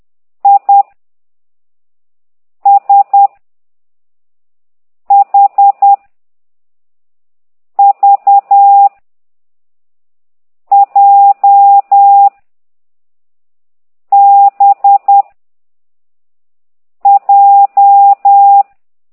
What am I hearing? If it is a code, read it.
Morse code, heard as a keyed tone, about 10 wpm, 800 Hz. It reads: I S H V J B J
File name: ct2idw / exerc15